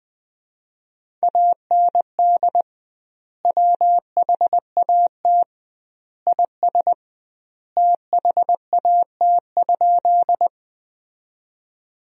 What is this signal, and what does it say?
Morse code, heard as a keyed tone, about 20 wpm, 700 Hz. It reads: AND WHAT IS THAT?
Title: 2x Word-Spacing - Sentences from Top 100 Words - 20wpm